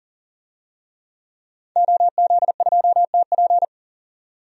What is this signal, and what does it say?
Morse code, heard as a keyed tone, about 40 wpm, 700 Hz. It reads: OZ2TP
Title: DXCC-100 Call Signs - 5 Characters - 40wpm